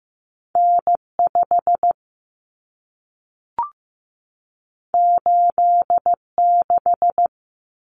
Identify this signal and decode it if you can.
Morse code, heard as a keyed tone, about 15 wpm, 700 Hz. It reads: N5 86
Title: Common Cut Numbers - 2-Digit 15wpm